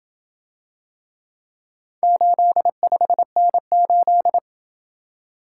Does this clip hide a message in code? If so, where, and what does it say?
Yes: Morse code from 2.0 to 4.4 s, heard as a keyed tone, about 27 wpm, 700 Hz. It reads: 85N8